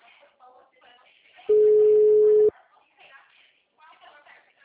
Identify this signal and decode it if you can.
Ringback tone (Europe, 425 Hz), with speech.